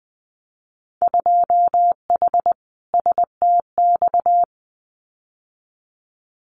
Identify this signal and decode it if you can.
Morse code, heard as a keyed tone, about 20 wpm, 700 Hz. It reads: 2H STX
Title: Winter Field Day Exchanges Encore 20wpm